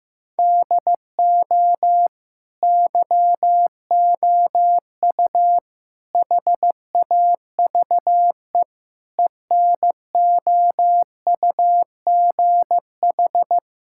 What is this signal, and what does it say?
Morse code, heard as a keyed tone, about 15 wpm, 700 Hz. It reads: DO YOU HAVE ENOUGH
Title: Sentences from Top 1000 Words - Review Part 5 - 15wpm